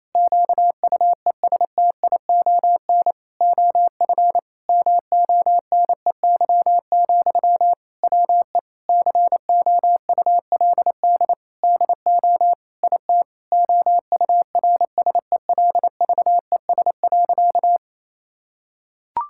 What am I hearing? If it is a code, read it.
Morse code, heard as a keyed tone, about 28 wpm, 700 Hz. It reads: QUESTION OF MONEY, WE COULD DO IT OURSELVES.
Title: Sentences from Top 1600 Words 28wpm